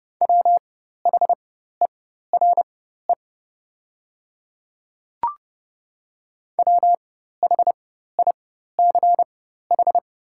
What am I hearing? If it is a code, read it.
Morse code, heard as a keyed tone, about 30 wpm, spaced as for 14 wpm, 700 Hz. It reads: WHERE WHICH
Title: 4x Character Spacing - Top 200 Words - 30wpm